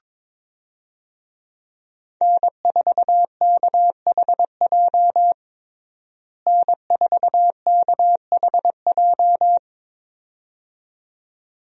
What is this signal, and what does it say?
Morse code, heard as a keyed tone, about 22 wpm, 700 Hz. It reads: N4KHJ N4KHJ
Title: USA Call Signs - 5 Characters - Repeated 2x - 22wpm